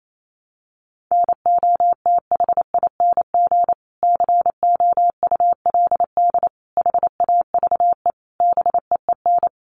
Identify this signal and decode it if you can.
Morse code, heard as a keyed tone, about 28 wpm, 700 Hz. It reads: NOTHING COULD HAVE BEEN
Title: Sentences from Top 500 Words 28wpm